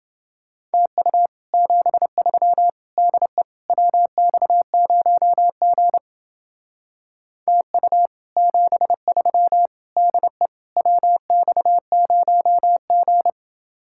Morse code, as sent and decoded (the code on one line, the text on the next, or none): - ..- / --... ...-- / -.. / .-- -..- ----- --. / - ..- / --... ...-- / -.. / .-- -..- ----- --.
TU 73 DE WX0G TU 73 DE WX0G